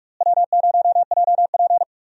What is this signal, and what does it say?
Morse code, heard as a keyed tone, about 45 wpm, 700 Hz. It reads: W0JP
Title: USA Call Signs - 4 Characters - Repeated 2x - 45wpm